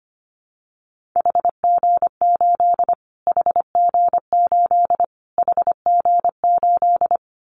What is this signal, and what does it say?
Morse code, heard as a keyed tone, about 25 wpm, 700 Hz. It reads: HG8 HG8 HG8
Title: Callsign Prefixes for Top 25 DXCC - Repeated 3x - 25wpm